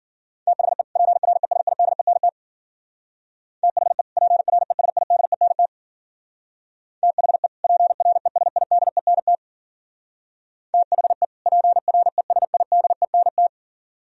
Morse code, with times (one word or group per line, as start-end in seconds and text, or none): 0.5-0.8 THE
1.0-2.3 PRESIDENT
3.6-4.0 THE
4.2-5.7 PRESIDENT
7.0-7.5 THE
7.6-9.3 PRESIDENT
10.7-11.2 THE
11.5-13.5 PRESIDENT